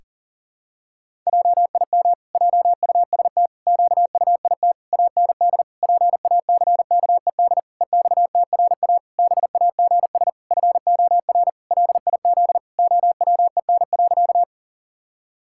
Morse code, heard as a keyed tone, about 37 wpm, 700 Hz. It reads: JIM JUST QUIT AND PACKED EXTRA BAGS FOR LIZ OWEN.
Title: Pangram Challenge v2 - 40wpm